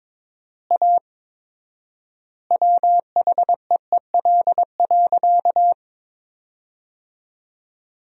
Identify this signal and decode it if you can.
Morse code, heard as a keyed tone, about 22 wpm, 700 Hz. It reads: A WHEEL.
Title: 4x Word-Spacing - Sentences from Top 500 Words - 22wpm